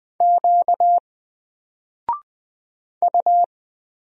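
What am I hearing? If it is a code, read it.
Morse code, heard as a keyed tone, about 20 wpm, 700 Hz. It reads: Q U
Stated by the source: Single letter and number Morse drill at 20 wpm, from N4LCD.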